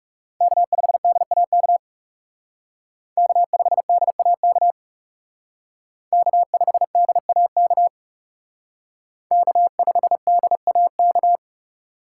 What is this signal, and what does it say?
Morse code, heard as a keyed tone, about 36 wpm, 700 Hz. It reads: K5DAK K5DAK K5DAK K5DAK
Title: Speed Racing - USA 1x3 Call Signs 30wpm